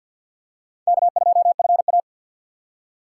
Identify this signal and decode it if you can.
Morse code, heard as a keyed tone, about 48 wpm, 700 Hz. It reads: K2FA
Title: USA Call Signs - 6x Character Spacing - 50wpm